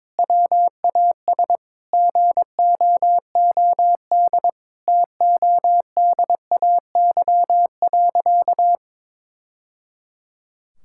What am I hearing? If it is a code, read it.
Morse code, heard as a keyed tone, about 21 wpm, 700 Hz. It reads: WAS GOOD TODAY.
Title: Sentences from Top 1000 Words - Review Part 5 - 22wpm